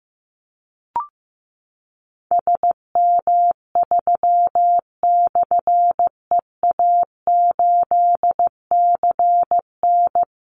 Morse code, heard as a keyed tone, about 15 wpm, 700 Hz. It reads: SM3/EA8CN
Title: Call Signs Repeated 2x - 15wpm